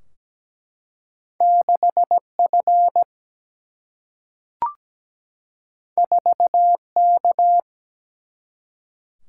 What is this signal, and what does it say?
Morse code, heard as a keyed tone, about 17 wpm, 700 Hz. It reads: 6 F 4 K
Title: Single-Letter Single-Number 17wpm